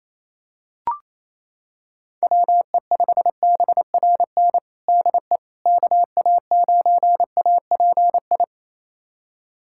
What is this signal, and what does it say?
Morse code, heard as a keyed tone, about 28 wpm, 700 Hz. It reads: WE5BRN DE KA9API
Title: USA Callsign DE Callsign - 6 Characters - 28wpm